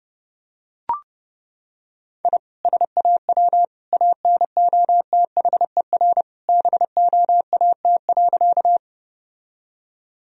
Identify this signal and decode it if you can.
Morse code, heard as a keyed tone, about 30 wpm, 700 Hz. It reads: I SAW ANOTHER BOAT.
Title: Sentences from Top 1000 Words - Review Part 3 - 30wpm